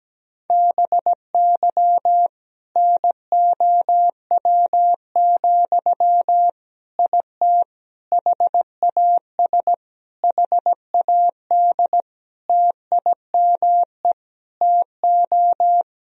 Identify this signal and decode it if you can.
Morse code, heard as a keyed tone, about 17 wpm, 700 Hz. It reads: BY NOW, IT HAS HAD TIME TO
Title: Sentences from Top 1000 Words - Review Part 1 - 17wpm